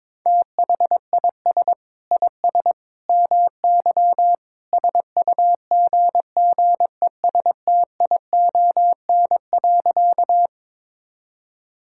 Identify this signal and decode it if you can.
Morse code, heard as a keyed tone, about 22 wpm, 700 Hz. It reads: THIS IS MY SUGGESTION.
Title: Sentences from Top 1600 Words 22wpm